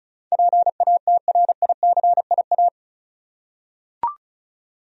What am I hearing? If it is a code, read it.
Morse code, heard as a keyed tone, about 34 wpm, 700 Hz. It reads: PATRICIA
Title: USA Common Names - 35wpm